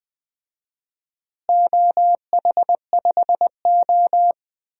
Morse code, heard as a keyed tone, about 20 wpm, 700 Hz. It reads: OH5O
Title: DXCC-100 Call Signs - 4 Characters - Repeated 2x - 20wpm